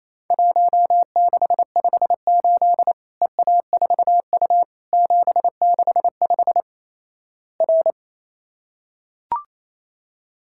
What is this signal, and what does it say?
Morse code, heard as a keyed tone, about 28 wpm, 700 Hz. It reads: 1658 EA4U 765 R
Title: Worked All Europe DX Contest - QTC Exchange - 28wpm